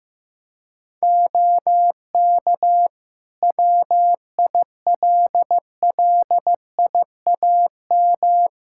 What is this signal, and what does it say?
Morse code, heard as a keyed tone, about 15 wpm, 700 Hz. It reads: OK WILLIAM
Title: Practice QSO - Exchange 2 - Info - 15wpm